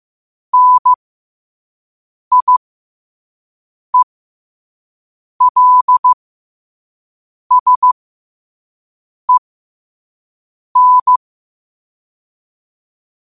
Morse code, heard as a keyed tone, about 15 wpm, spaced as for 5 wpm, 1000 Hz. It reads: NIELSEN